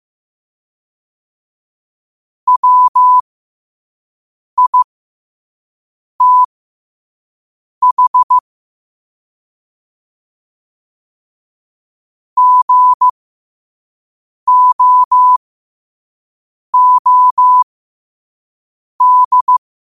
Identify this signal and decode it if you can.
Morse code, heard as a keyed tone, about 15 wpm, spaced as for 5 wpm, 1000 Hz. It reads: WITH GOOD
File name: qotdfiles / 5wpm